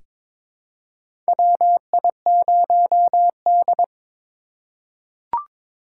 Morse code, heard as a keyed tone, about 22 wpm, 700 Hz. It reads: WI0D
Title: USA Call Signs - 4 Characters - 4x Character Spacing - 22wpm